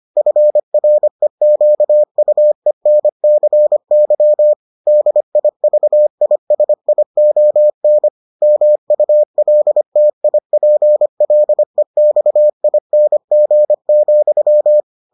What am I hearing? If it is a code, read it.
Morse code, heard as a keyed tone, about 25 wpm, 600 Hz. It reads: FREQUENCY DIVISION MULTIPLEXING,